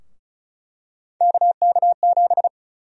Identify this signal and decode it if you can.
Morse code, heard as a keyed tone, about 36 wpm, 700 Hz. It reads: KK7
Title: USA Call Sign Prefixes - Repeated 2x - 35wpm